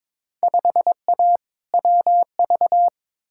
Morse code, heard as a keyed tone, about 23 wpm, 700 Hz. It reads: 5A WV
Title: Field Day Exchanges - 22wpm